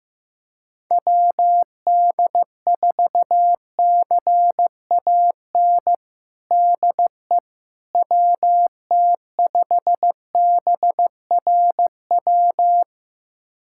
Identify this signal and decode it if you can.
Morse code, heard as a keyed tone, about 15 wpm, 700 Hz. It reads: WD4CAN DE WT5BRW
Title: USA Callsign DE Callsign 15wpm